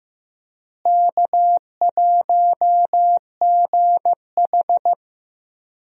Morse code, heard as a keyed tone, about 15 wpm, 700 Hz. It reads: K1GH
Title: USA Call Signs - 4 Characters - Repeated 3x - 15wpm